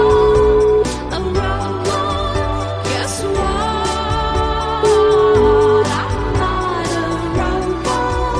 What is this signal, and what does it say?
Ringback tone (Europe, 425 Hz), with noise.